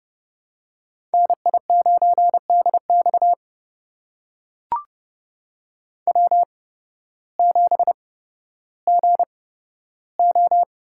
Morse code, heard as a keyed tone, about 30 wpm, spaced as for 14 wpm, 700 Hz. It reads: NI9DX W7GO
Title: USA Call Signs - 8x Character Spacing - 30wpm